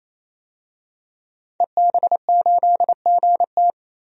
Morse code, heard as a keyed tone, about 28 wpm, 700 Hz. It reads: EB8GT
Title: DXCC-100 Call Signs - 5 Characters - 28wpm